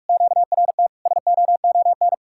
Morse code, heard as a keyed tone, about 49 wpm, 700 Hz. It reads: QRT SOON